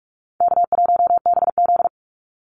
Morse code, heard as a keyed tone, about 44 wpm, 700 Hz. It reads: K1BZ
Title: Call Signs - 45wpm